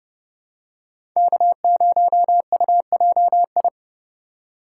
Morse code, heard as a keyed tone, about 30 wpm, 700 Hz. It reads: K0UJI
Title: USA Call Signs - 5 Characters - 4x Character Spacing - 30wpm